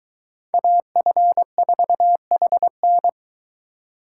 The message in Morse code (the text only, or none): AF4HN